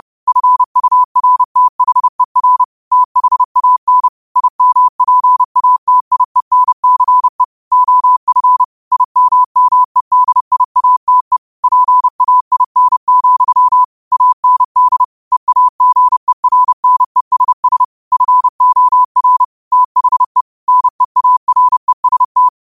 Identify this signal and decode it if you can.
Morse code, heard as a keyed tone, about 30 wpm, 1000 Hz. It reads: FURTHER THAN IMPATIENCE OF IMMEDIATE PAIN, AND EAGERNESS FOR THE NEAREST
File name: qotdfiles / 30wpm